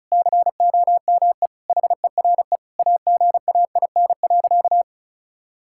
Morse code, heard as a keyed tone, about 34 wpm, 700 Hz. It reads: COME HERE AGAIN.
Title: Sentences from Top 1600 Words 35wpm